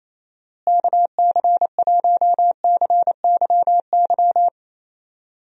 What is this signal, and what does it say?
Morse code, heard as a keyed tone, about 28 wpm, 700 Hz. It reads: KC1CYY
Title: USA Call Signs - 6 Characters - Repeated 2x - 28wpm